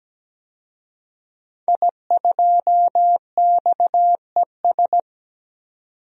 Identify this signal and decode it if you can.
Morse code, heard as a keyed tone, about 17 wpm, 700 Hz. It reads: I2XES
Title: DXCC-100 Call Signs - 5 Characters - 17wpm